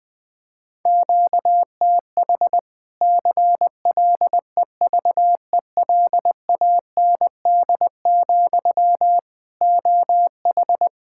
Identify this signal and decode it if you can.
Morse code, heard as a keyed tone, about 20 wpm, 700 Hz. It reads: QTH CLEVELAND, OH
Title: Practice QSO - Exchange 1 - Beginning - 20wpm